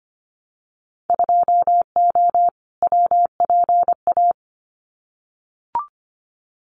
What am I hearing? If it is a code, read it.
Morse code, heard as a keyed tone, about 25 wpm, 700 Hz. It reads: 2O WPA
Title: Winter Field Day Exchanges Encore 25wpm